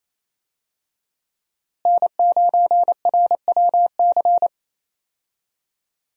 Morse code, heard as a keyed tone, about 28 wpm, 700 Hz. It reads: N9RWC
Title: USA Call Signs - 5 Characters - Repeated 3x - 28wpm